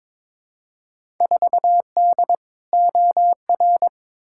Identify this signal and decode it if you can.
Morse code, heard as a keyed tone, about 22 wpm, 700 Hz. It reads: 4D OR
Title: Field Day Exchanges Encore - 22wpm